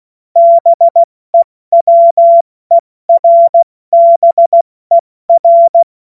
Morse code, heard as a keyed tone, about 16 wpm, spaced as for 14 wpm, 680 Hz. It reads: BEWERBER
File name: Morsekurs / Witze / Witz 11. April 70 F80